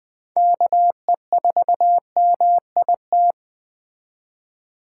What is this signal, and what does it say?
Morse code, heard as a keyed tone, about 20 wpm, 700 Hz. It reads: KE4MIT